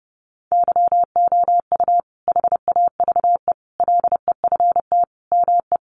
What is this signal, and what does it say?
Morse code, heard as a keyed tone, about 30 wpm, 700 Hz. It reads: YOU HAVE LEFT ME
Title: Sentences from Top 600 Words 30wpm